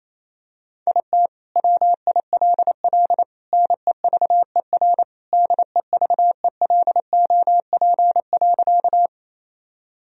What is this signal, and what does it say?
Morse code, heard as a keyed tone, about 27 wpm, 700 Hz. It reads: IT WILL NEVER DEVELOP.
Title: Sentences from Top 1000 Words - Review Part 2 - 28wpm